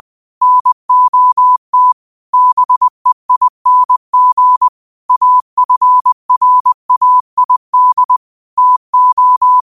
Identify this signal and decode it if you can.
Morse code, heard as a keyed tone, about 20 wpm, 1000 Hz. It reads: NOT BEING AFRAID TO
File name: qotdfiles / 20wpm